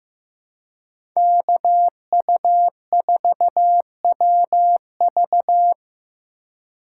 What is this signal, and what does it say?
Morse code, heard as a keyed tone, about 15 wpm, 700 Hz. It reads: KU4WV
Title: USA Call Signs - 5 Characters - 3x Character Spacing - 15wpm